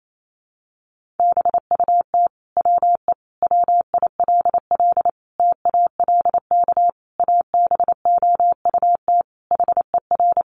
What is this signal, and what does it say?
Morse code, heard as a keyed tone, about 28 wpm, 700 Hz. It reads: BUT WE WILL TALK ABOUT HER
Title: Sentences from Top 1200 Words 28wpm